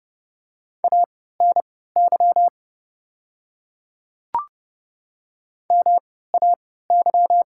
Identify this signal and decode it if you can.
Morse code, heard as a keyed tone, about 32 wpm, spaced as for 17 wpm, 690 Hz. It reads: ANY MAY
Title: 3x Character Spacing - Top 500 Words - 30wpm